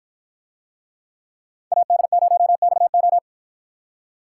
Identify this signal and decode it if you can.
Morse code, heard as a keyed tone, about 50 wpm, 700 Hz. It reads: AD0XO